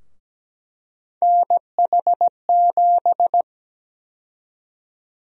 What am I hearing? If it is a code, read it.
Morse code, heard as a keyed tone, about 17 wpm, 700 Hz. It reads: NH7
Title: USA Call Sign Prefixes - Repeated 2x - 17wpm